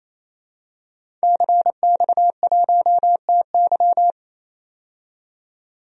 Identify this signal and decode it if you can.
Morse code, heard as a keyed tone, about 28 wpm, 700 Hz. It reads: CX1TY
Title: DXCC-100 Call Signs - 5 Characters - 28wpm